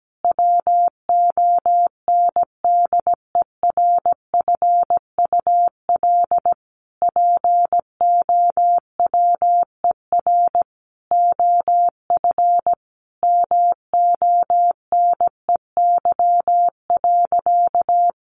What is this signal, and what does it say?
Morse code, heard as a keyed tone, about 17 wpm, 700 Hz. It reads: WONDERFUL POWER OF MONEY.
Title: Sentences from Top 1100 Words 17wpm